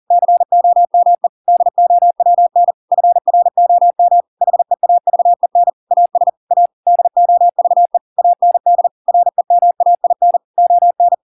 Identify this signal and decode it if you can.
Morse code, heard as a keyed tone, about 40 wpm, 700 Hz. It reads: COME DOWN FROM HEAVEN AS A DOVE AND REMAIN ON